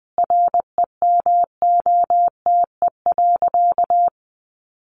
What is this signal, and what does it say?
Morse code, heard as a keyed tone, about 20 wpm, 700 Hz. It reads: REMOTE.
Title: Sentences from Top 1300 Words 20wpm